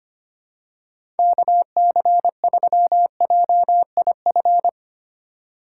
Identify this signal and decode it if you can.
Morse code, heard as a keyed tone, about 25 wpm, 700 Hz. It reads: KC3JIF
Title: USA Call Signs - 6 Characters - 4x Character Spacing - 25wpm